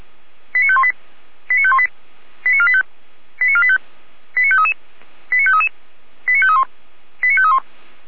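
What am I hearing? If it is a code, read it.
Five-tone selective call (ZVEI), heard as quick runs of five tones, tones of about 70 ms. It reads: 89518 89518 89485 89485 89533 89533 89531 89531
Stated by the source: ZVEI five-tone selective call received on SDR, feuerwehr-orlamuende.org fire brigade archive.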